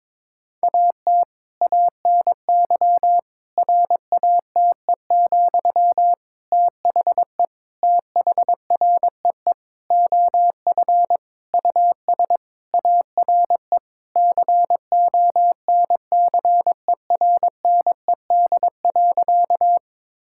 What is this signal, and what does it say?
Morse code, heard as a keyed tone, about 22 wpm, 700 Hz. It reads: AT ANY RATE, THE THREE OF US ARE CONCERNED.
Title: Sentences from Top 1000 Words - Review Part 6 - 22wpm